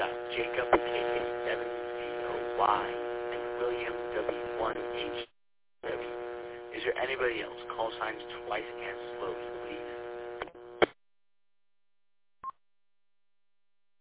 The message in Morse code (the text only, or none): none